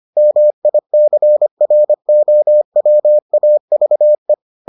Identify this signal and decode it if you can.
Morse code, heard as a keyed tone, about 25 wpm, 600 Hz. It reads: MICROWAVE